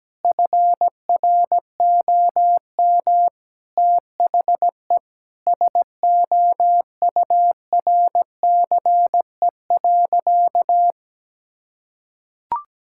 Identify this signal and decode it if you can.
Morse code, heard as a keyed tone, about 17 wpm, 700 Hz. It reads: FROM THE SOURCE.
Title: Sentences from Top 1000 Words - Review Part 5 - 17wpm